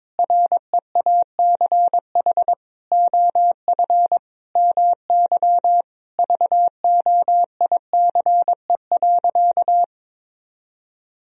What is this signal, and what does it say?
Morse code, heard as a keyed tone, about 22 wpm, 700 Hz. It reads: REACH OF MY VOICE.
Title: Sentences from Top 1000 Words - Review Part 4 - 22wpm